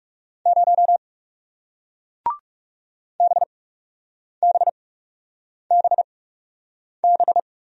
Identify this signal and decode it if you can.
Morse code, heard as a keyed tone, about 37 wpm, spaced as for 8 wpm, 700 Hz. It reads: 0 BBBB
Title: Speed-Racing Single Letter-Number 30wpm